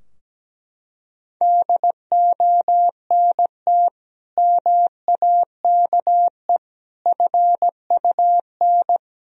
Morse code, heard as a keyed tone, about 17 wpm, 700 Hz. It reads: DONT MAKE FUN
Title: Sentences from Top 1000 Words - Review Part 4 - 17wpm